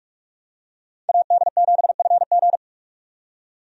Morse code, heard as a keyed tone, about 46 wpm, 700 Hz. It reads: AD7FG
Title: USA Call Signs - 5 Characters - 45wpm